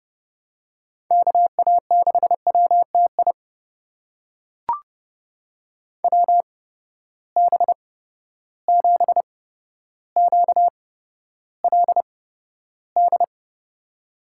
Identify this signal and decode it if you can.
Morse code, heard as a keyed tone, about 32 wpm, spaced as for 13 wpm, 700 Hz. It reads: KA6WTI WB7QLD
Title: USA Call Signs - 6 Characters - 8x Character Spacing - 30wpm